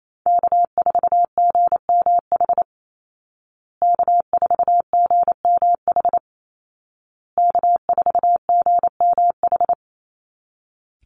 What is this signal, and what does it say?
Morse code, heard as a keyed tone, about 28 wpm, 700 Hz. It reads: K4GMH K4GMH K4GMH
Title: Call Signs Repeated 2x - 28wpm